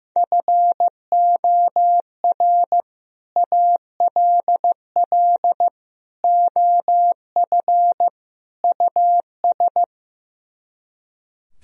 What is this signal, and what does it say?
Morse code, heard as a keyed tone, about 15 wpm, 700 Hz. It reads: FOR ALL OF US
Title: Speed-Racing - Sets of 4 Words 15wpm